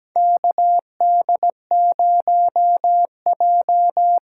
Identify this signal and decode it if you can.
Morse code, heard as a keyed tone, about 17 wpm, 700 Hz. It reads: KD0J
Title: USA Call Signs - 6 Characters - 8x Character Spacing - 17wpm